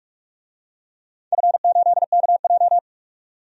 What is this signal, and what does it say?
Morse code, heard as a keyed tone, about 46 wpm, 700 Hz. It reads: F8KJ
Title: DXCC-100 Call Signs - 4 Characters - 45wpm